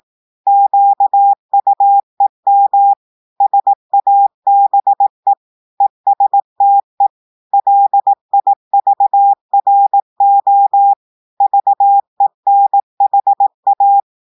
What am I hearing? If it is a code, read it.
Morse code, heard as a keyed tone, about 18 wpm, 800 Hz. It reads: QUEM SABE ESTE LIVRO VENHA
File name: audios / portugais / portugais texte en clair 1 vitesse 18 mots minutes